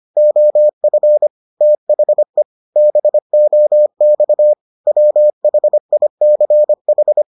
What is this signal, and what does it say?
Morse code, heard as a keyed tone, about 25 wpm, 600 Hz. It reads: OF THE BOX WHICH